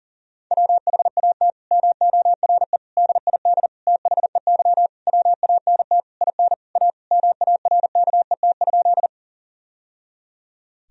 Morse code, heard as a keyed tone, about 37 wpm, 700 Hz. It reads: WHAT MORE DID THEY WANT IN A MARKET?
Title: Sentences from Top 1000 Words - Review Part 3 - 40wpm